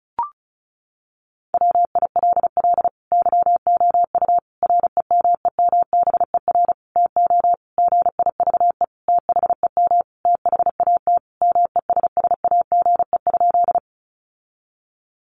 Morse code, heard as a keyed tone, about 34 wpm, 700 Hz. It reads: WILL YOU REMEMBER TO GIVE THEM THAT MESSAGE?
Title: Sentences from Top 1100 Words 35wpm